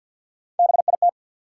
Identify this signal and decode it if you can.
Morse code, heard as a keyed tone, about 52 wpm, 700 Hz. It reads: BIT
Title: Sets of 3 Words - New Word Drill - 50wpm